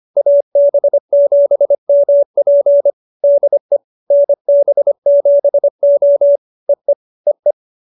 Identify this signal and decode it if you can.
Morse code, heard as a keyed tone, about 25 wpm, 570 Hz. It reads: AB7MP DE NB7O EE EE